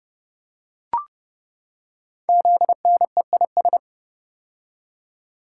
Morse code, heard as a keyed tone, about 30 wpm, 700 Hz. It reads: ZNEIS
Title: Code Groups - 5 Letters - 30wpm